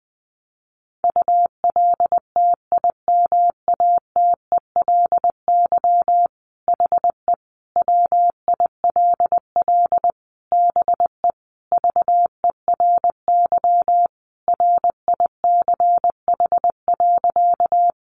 Morse code, heard as a keyed tone, about 20 wpm, 700 Hz. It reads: ULTIMATELY HE WILL BE VERY RICH.
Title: Sentences from Top 1200 Words 20wpm